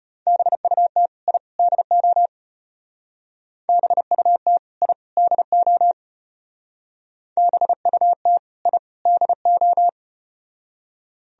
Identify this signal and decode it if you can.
Morse code, heard as a keyed tone, about 34 wpm, 700 Hz. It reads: BUT I DO BUT I DO BUT I DO